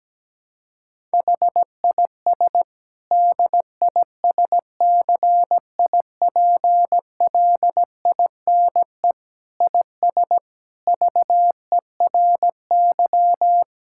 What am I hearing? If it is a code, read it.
Morse code, heard as a keyed tone, about 17 wpm, 700 Hz. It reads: HIS DISCIPLINE IS VERY
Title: Sentences from Top 1000 Words - Review Part 6 - 17wpm